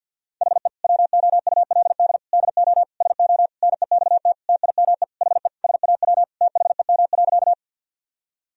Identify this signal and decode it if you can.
Morse code, heard as a keyed tone, about 52 wpm, 700 Hz. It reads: HE WOULD DO SO NEXT TIME HE SAW THEM.
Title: Sentences from Top 1000 Words - Review Part 1 - 50wpm